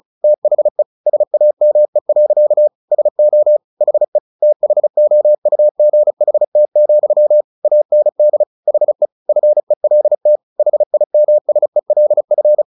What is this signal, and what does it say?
Morse code, heard as a keyed tone, about 34 wpm, 600 Hz. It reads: THE SAME. SO HE THOUGHT, AND HE FELT HIMSELF
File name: the old man and the sea 2 35wpm